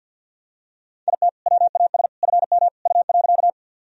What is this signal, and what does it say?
Morse code, heard as a keyed tone, about 46 wpm, 700 Hz. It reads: IT WAS FM U.
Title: Sentences from Top 100 Words and Abbreviations 50wpm